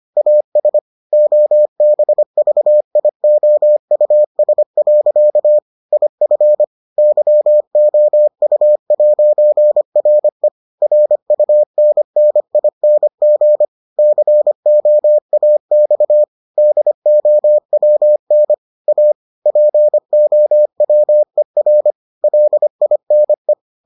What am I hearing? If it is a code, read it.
Morse code, heard as a keyed tone, about 25 wpm, 600 Hz. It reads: AS OBVIOUS. IF YOU'RE RUNNING COAX DOWN A POWER LINE